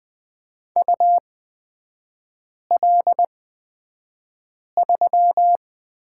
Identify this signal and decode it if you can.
Morse code, heard as a keyed tone, about 20 wpm, spaced as for 5 wpm, 700 Hz. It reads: UL3
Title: 250 Lesson 40e - BK - Three Characters 20wpm